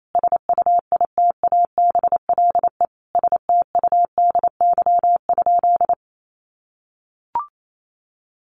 Morse code, heard as a keyed tone, about 28 wpm, 700 Hz. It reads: SUITABLE STUDY?
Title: Sentences from Top 1400 Words 28wpm